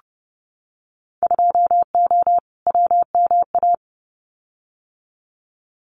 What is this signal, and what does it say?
Morse code, heard as a keyed tone, about 30 wpm, 700 Hz. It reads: 2O WMA
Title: Winter Field Day Exchanges Encore 30wpm